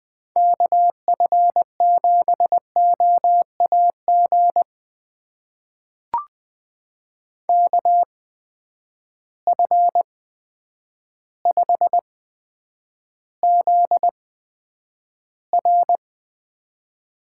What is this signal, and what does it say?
Morse code, heard as a keyed tone, about 20 wpm, spaced as for 9 wpm, 700 Hz. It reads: KF7OAG KF5ZR
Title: USA Call Signs - 6 Characters - 8x Character Spacing - 20wpm